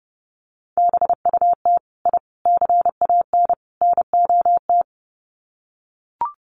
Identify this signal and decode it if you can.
Morse code, heard as a keyed tone, about 30 wpm, 700 Hz. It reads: BUT I CAN NOT
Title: Sets of 4 Words 30wpm